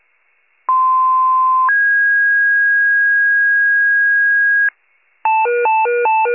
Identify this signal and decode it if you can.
Two-tone page, heard as a one-second tone, then a three-second tone, about 1040 then 1650 Hz. It is heard from 0.7 to 4.7 s.
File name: pager / 95 RESCUE DUTY / wcalert